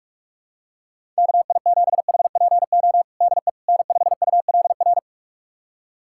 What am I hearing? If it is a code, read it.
Morse code, heard as a keyed tone, about 46 wpm, 700 Hz. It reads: KI7HPO DE N5ULR
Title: USA Callsign DE Callsign 45wpm